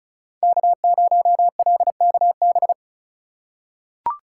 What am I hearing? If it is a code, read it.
Morse code, heard as a keyed tone, about 36 wpm, 700 Hz. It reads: K0LKB
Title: USA Call Signs - 5 Characters - 4x Character Spacing - 35wpm